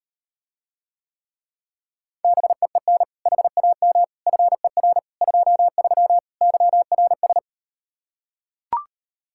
Morse code, heard as a keyed tone, about 39 wpm, 700 Hz. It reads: BEEN HAM FER 23 YRS